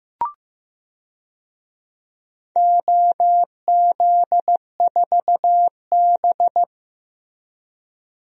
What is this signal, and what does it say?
Morse code, heard as a keyed tone, about 15 wpm, 700 Hz. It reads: OZ4B
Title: DXCC-100 Call Signs - 4 Characters - Repeated 2x - 15wpm